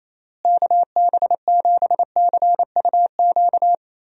Morse code, heard as a keyed tone, about 28 wpm, 700 Hz. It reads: KB7CUQ
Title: USA Call Signs - 6 Characters - Repeated 3x - 28wpm